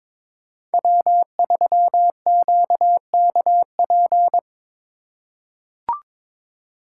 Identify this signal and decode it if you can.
Morse code, heard as a keyed tone, about 23 wpm, 700 Hz. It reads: W3QKP
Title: USA Call Signs - 4x Character Spacing - 22wpm